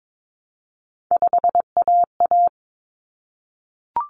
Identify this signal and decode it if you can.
Morse code, heard as a keyed tone, about 22 wpm, 700 Hz. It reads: 5AA
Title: Common Cut Numbers - 3-Digit 22wpm